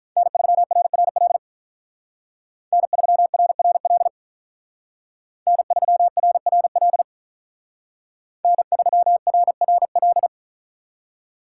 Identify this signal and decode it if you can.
Morse code, heard as a keyed tone, about 43 wpm, 700 Hz. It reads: N3RRL N3RRL N3RRL N3RRL
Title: Speed Racing - USA 1x3 Call Signs 35wpm